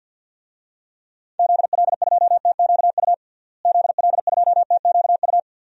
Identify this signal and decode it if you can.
Morse code, heard as a keyed tone, about 50 wpm, 700 Hz. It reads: ZL2TQU ZL2TQU
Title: DXCC-100 Call Signs - 6 Characters - Repeated 2x - 50wpm